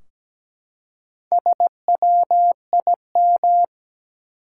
Morse code, heard as a keyed tone, about 17 wpm, 700 Hz. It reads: SWIM
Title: Four Letter Words - 17wpm